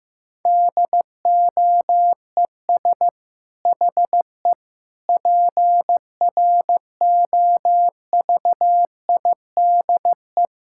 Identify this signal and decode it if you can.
Morse code, heard as a keyed tone, about 15 wpm, 700 Hz. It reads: DOES HE PROVIDE
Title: Sentences from Top 1000 Words - Review Part 6 - 15wpm